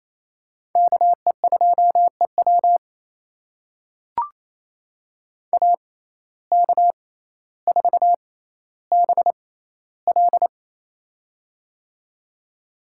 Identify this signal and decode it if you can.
Morse code, heard as a keyed tone, about 28 wpm, spaced as for 14 wpm, 700 Hz. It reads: KE2EW AK4BL
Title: USA Call Signs - 5 Characters - 6x Character Spacing - 28wpm